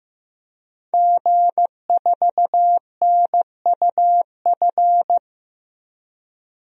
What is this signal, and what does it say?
Morse code, heard as a keyed tone, about 15 wpm, 700 Hz. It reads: G4NUF
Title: DXCC-100 Call Signs - 5 Characters - 15wpm